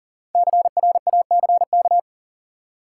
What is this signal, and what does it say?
Morse code, heard as a keyed tone, about 40 wpm, 700 Hz. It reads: CRACK
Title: Five Letter Words - 40wpm